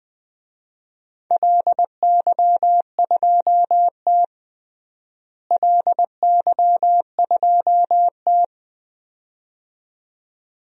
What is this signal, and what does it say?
Morse code, heard as a keyed tone, about 20 wpm, 700 Hz. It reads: LY2T LY2T
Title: DXCC-100 Call Signs - 4 Characters - Repeated 2x - 20wpm